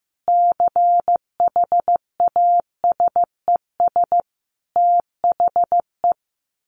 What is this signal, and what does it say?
Morse code, heard as a keyed tone, about 15 wpm, 700 Hz. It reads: CHASES THE
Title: Additional Easy Phrases 15wpm